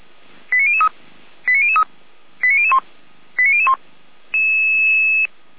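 Five-tone selective call (ZVEI), heard as quick runs of five tones, tones of about 70 ms. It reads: 89003 89003 89001 89001
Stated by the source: ZVEI five-tone call 89503 received by SDR, archive path 455.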